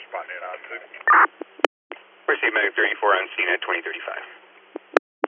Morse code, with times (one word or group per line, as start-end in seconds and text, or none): none